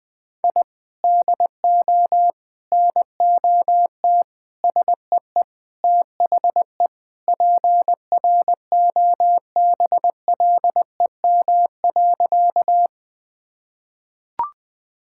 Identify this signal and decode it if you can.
Morse code, heard as a keyed tone, about 20 wpm, 700 Hz. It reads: I DO NOT SEE THE PROBLEM.
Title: Sentences with 4-Word Phrases - 20wpm